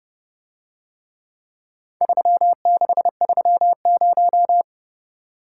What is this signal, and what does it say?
Morse code, heard as a keyed tone, about 30 wpm, 700 Hz. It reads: 3630